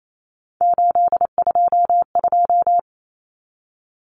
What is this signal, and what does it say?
Morse code, heard as a keyed tone, about 28 wpm, 700 Hz. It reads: 822
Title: Numbers - 3-Digit 28wpm